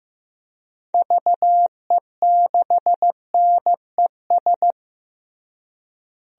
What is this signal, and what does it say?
Morse code, heard as a keyed tone, about 15 wpm, 700 Hz. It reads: VE6NES
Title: DXCC-100 Call Signs - 6 Characters - Repeated 3x - 15wpm